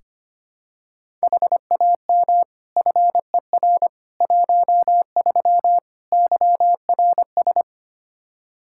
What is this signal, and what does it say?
Morse code, heard as a keyed tone, about 25 wpm, 700 Hz. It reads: HAM FER 13 YRS
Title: Practice QSO - Exchange 3 - More Info - 25wpm